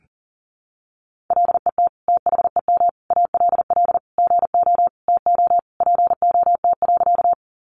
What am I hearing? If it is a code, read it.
Morse code, heard as a keyed tone, about 40 wpm, 700 Hz. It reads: LET THEM ALL GO TO POT.
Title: Sentences from Top 1000 Words 40wpm